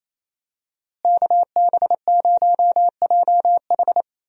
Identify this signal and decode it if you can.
Morse code, heard as a keyed tone, about 28 wpm, 700 Hz. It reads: KB0JH
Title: USA Call Signs - 5 Characters - Repeated 3x - 28wpm